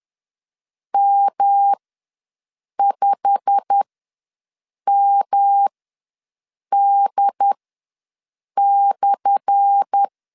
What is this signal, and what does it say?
Morse code, heard as a keyed tone, about 10 wpm, spaced as for 6 wpm, 800 Hz. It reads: M5MD/
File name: LEZIONI / 10 SIGNS / 10A SLASH ws 10wpm 4121